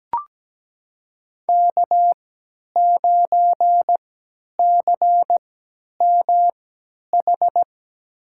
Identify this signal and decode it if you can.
Morse code, heard as a keyed tone, about 17 wpm, spaced as for 10 wpm, 700 Hz. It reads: K9CMH
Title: USA Call Signs - 5 Characters - 3x Character Spacing - 17wpm